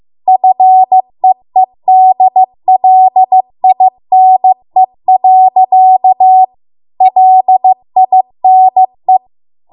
Morse code, heard as a keyed tone, about 15 wpm, 750 Hz. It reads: FEEDLINE. LINE